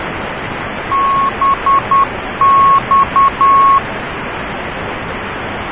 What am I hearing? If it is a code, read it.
Morse code, heard as a keyed tone, about 10 wpm, 1100 Hz, with noise. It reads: BX